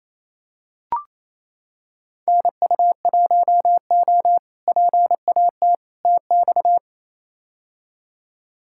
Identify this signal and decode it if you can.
Morse code, heard as a keyed tone, about 28 wpm, 700 Hz. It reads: NU1O PAT TX
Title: K1USN SST QSO Practice - 28wpm